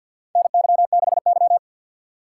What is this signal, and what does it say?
Morse code, heard as a keyed tone, about 52 wpm, 700 Hz. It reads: NY6Y
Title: K1USN SST QSO Practice - 50wpm